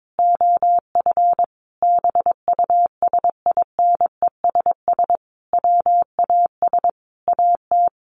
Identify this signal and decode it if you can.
Morse code, heard as a keyed tone, about 22 wpm, 700 Hz. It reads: OF BUSINESS WAS AT
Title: Sentences from Top 1300 Words 22wpm